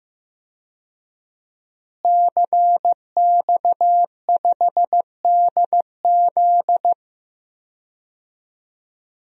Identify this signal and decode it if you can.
Morse code, heard as a keyed tone, about 15 wpm, 700 Hz. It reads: CX5DZ
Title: DXCC-100 Call Signs - 5 Characters - Repeated 3x - 15wpm